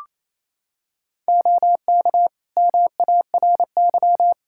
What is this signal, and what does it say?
Morse code, heard as a keyed tone, about 28 wpm, 700 Hz. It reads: OK MARY
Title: Practice QSO - Exchange 3 - More Info - 28wpm